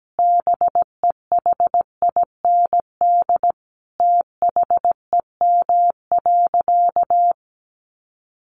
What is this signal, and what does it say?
Morse code, heard as a keyed tone, about 17 wpm, 700 Hz. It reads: BEHIND THEM.
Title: Sentences from Top 1400 Words 17wpm